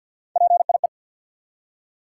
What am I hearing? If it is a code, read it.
Morse code, heard as a keyed tone, about 53 wpm, 700 Hz. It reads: PIE
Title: Three Letter Words - 50wpm